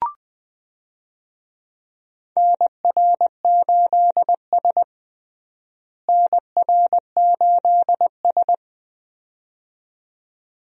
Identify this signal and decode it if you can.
Morse code, heard as a keyed tone, about 20 wpm, 700 Hz. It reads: NR8S NR8S
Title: USA Call Signs - 4 Characters - Repeated 2x - 20wpm